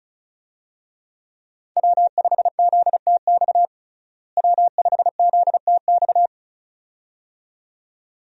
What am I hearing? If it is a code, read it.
Morse code, heard as a keyed tone, about 34 wpm, 700 Hz. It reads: W5ZTX W5ZTX
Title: USA Call Signs - 5 Characters - Repeated 2x - 35wpm